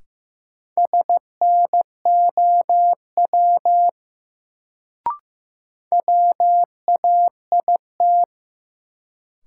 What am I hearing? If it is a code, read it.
Morse code, heard as a keyed tone, about 15 wpm, 700 Hz. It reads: SNOW WAIT
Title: Sentences from Top 500 Words - New Word Drill - 15wpm